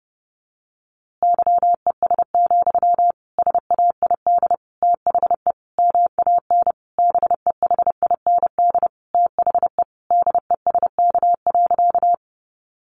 Morse code, heard as a keyed tone, about 28 wpm, 700 Hz. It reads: YES, SAID THE MAN BEHIND THE DESK.
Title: A Sound Of Thunder 30wpm